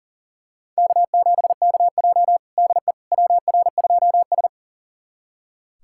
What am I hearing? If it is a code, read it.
Morse code, heard as a keyed tone, about 40 wpm, 700 Hz. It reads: K7KJ DE WR2S
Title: USA Callsign DE Callsign - 4 Characters - 40wpm